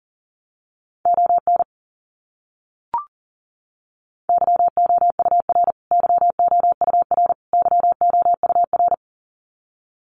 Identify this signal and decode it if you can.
Morse code, heard as a keyed tone, about 40 wpm, 700 Hz. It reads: ON YOUR YOUR YOUR
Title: Top 200 Words Repeated 40wpm